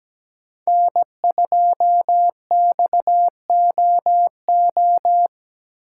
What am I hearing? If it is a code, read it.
Morse code, heard as a keyed tone, about 17 wpm, 700 Hz. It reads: N2XOO
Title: USA Call Signs - 5 Characters - 4x Character Spacing - 17wpm